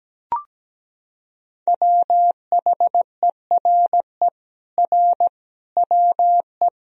Morse code, heard as a keyed tone, about 17 wpm, 700 Hz. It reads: WHERE R WE
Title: Sentences from Top 200 Words and Abbreviations 17wpm